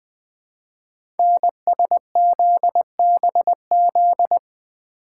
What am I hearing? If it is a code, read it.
Morse code, heard as a keyed tone, about 20 wpm, 700 Hz. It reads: NSZBZ
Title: Code Groups - 5 Letters - 20wpm